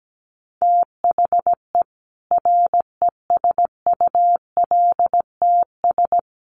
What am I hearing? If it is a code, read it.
Morse code, heard as a keyed tone, about 17 wpm, 700 Hz. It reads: THE RESULTS
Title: Sets of 5 Words 17wpm